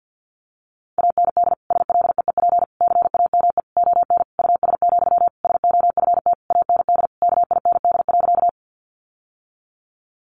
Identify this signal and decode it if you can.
Morse code, heard as a keyed tone, about 50 wpm, 700 Hz. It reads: AND SLEEP CAME ON US, SOFT AND KIND.
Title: Sentences from Top 1500 Words 50wpm